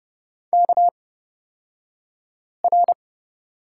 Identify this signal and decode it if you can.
Morse code, heard as a keyed tone, about 30 wpm, spaced as for 5 wpm, 700 Hz. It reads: KR
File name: Lesson 37d - Letter x - Two Characters 30wpm